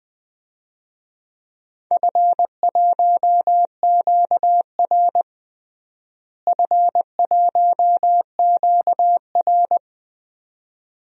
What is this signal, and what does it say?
Morse code, heard as a keyed tone, about 20 wpm, 700 Hz. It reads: F1QR F1QR
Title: DXCC-100 Call Signs - 4 Characters - Repeated 2x - 20wpm